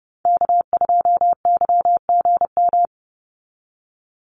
Morse code, heard as a keyed tone, about 30 wpm, 700 Hz. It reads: K2YGM
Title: Call Signs - 30wpm